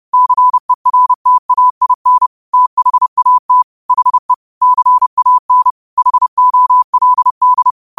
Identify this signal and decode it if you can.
Morse code, heard as a keyed tone, about 30 wpm, 1000 Hz. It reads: CERTAIN THAT HE CAN HOLD
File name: qotdfiles / 30wpm